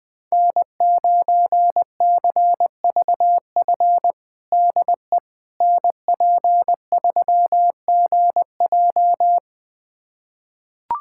Morse code, heard as a keyed tone, about 21 wpm, 700 Hz. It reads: N9CVF DE NP3GJ
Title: USA Callsign DE Callsign - 5 Characters - 20wpm